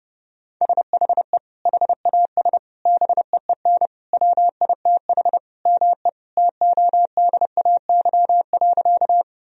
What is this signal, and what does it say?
Morse code, heard as a keyed tone, about 30 wpm, 700 Hz. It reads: SHE HAS BEEN WITH ME TODAY.
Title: Sentences from Top 1000 Words - Review Part 4 - 30wpm